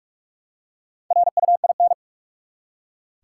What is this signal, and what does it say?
Morse code, heard as a keyed tone, about 46 wpm, 700 Hz. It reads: RUIN
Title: Four Letter Words - 45wpm